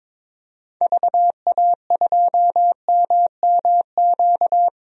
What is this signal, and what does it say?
Morse code, heard as a keyed tone, about 22 wpm, 700 Hz. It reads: VA2MMQ
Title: DXCC-100 Call Signs - 6 Characters - 22wpm